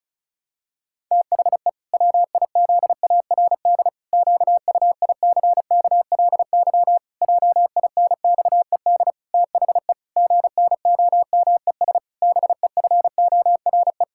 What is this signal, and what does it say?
Morse code, heard as a keyed tone, about 36 wpm, 700 Hz. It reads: THE WIZARD QUICKLY JINXED THE GNOMES BEFORE
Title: Pangram Challenge v2 - 35wpm